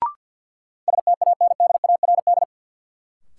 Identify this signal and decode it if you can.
Morse code, heard as a keyed tone, about 50 wpm, 700 Hz. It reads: STANDARD
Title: Sentences from Top 1000 Words - Review Part 6 - New Word Drill - 50wpm